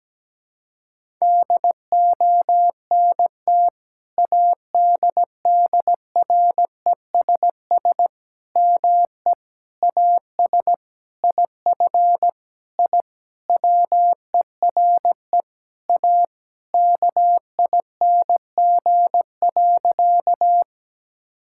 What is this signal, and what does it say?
Morse code, heard as a keyed tone, about 17 wpm, 700 Hz. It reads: DONT ADDRESS ME AS IF I WERE A KING.
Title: Sentences from Top 1000 Words - Review Part 6 - 17wpm